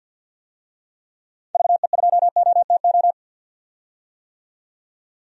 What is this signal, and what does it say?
Morse code, heard as a keyed tone, about 48 wpm, 700 Hz. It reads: VE2OTO
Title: DXCC-100 Call Signs - 6 Characters - 50wpm